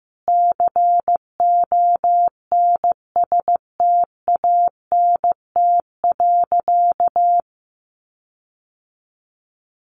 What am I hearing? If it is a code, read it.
Morse code, heard as a keyed tone, about 15 wpm, 700 Hz. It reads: CONSTANT.
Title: Sentences from Top 1500 Words 15wpm